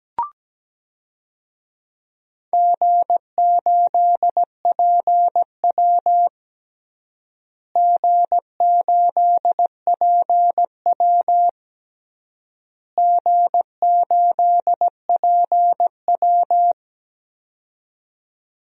Morse code, heard as a keyed tone, about 17 wpm, 700 Hz. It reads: G8PW G8PW G8PW
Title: DXCC-100 Call Signs - 4 Characters - Repeated 3x - 17wpm